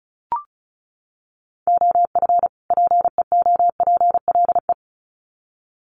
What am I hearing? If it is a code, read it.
Morse code, heard as a keyed tone, about 34 wpm, 700 Hz. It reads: OF PEOPLE
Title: Sets of 2 Words 35wpm